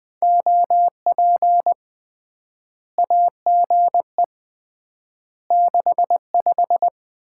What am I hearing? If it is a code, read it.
Morse code, heard as a keyed tone, about 21 wpm, 700 Hz. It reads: OP AGE 65
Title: Common QSO Phrases - 3x Word-Spacing - 20wpm